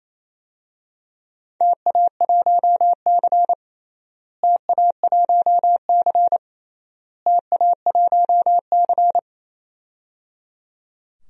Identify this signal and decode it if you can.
Morse code, heard as a keyed tone, about 28 wpm, 700 Hz. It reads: TA1C TA1C TA1C
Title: DXCC-100 Call Signs - 4 Characters - Repeated 3x - 28wpm